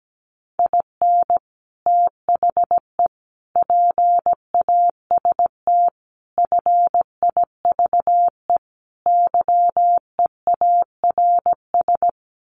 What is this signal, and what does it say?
Morse code, heard as a keyed tone, about 17 wpm, 700 Hz. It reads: IN THE PAST FIVE YEARS
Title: Sets of 5 Words Encore 17wpm